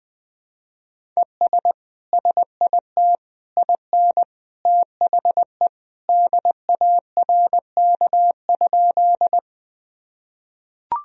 Morse code, heard as a keyed tone, about 21 wpm, 700 Hz. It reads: ES SIT IN THE DARK?
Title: Sentences from Top 500 Words and Abbreviations 20wpm